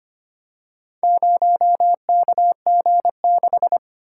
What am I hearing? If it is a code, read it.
Morse code, heard as a keyed tone, about 25 wpm, 700 Hz. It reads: 0KG6
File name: practice / 25wpm / Four Letter-Number Combinations - 25wpm